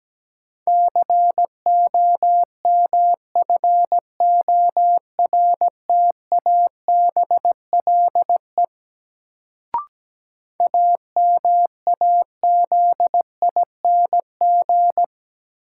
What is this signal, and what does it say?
Morse code, heard as a keyed tone, about 17 wpm, 700 Hz. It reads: COMFORTABLE AMAZING
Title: Sentences from Top 1500 Words - New Word Drill Alt - 17wpm